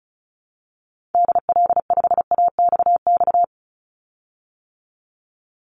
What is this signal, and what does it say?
Morse code, heard as a keyed tone, about 34 wpm, 700 Hz. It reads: DL5AXX
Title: Call Signs - 35wpm